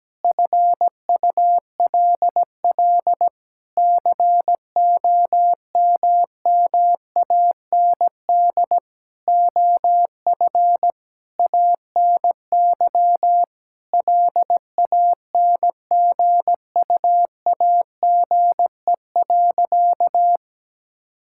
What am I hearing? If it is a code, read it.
Morse code, heard as a keyed tone, about 17 wpm, 700 Hz. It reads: FULL COMMAND OF ANY LANGUAGE.
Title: Sentences from Top 1600 Words 17wpm